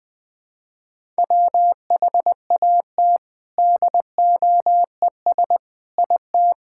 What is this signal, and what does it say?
Morse code, heard as a keyed tone, about 20 wpm, 700 Hz. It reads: WHAT DOES IT
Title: Sentences from Top 1000 Words - Review Part 5 - 20wpm